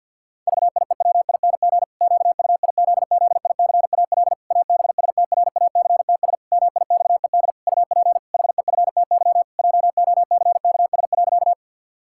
Morse code, heard as a keyed tone, about 50 wpm, 700 Hz. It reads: VIEWING QUIZZICAL ABSTRACTS MIXED UP HEFTY JOCKS.
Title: Pangram Challenge v2 - 50wpm